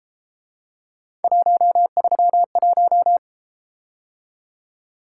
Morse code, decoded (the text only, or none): 131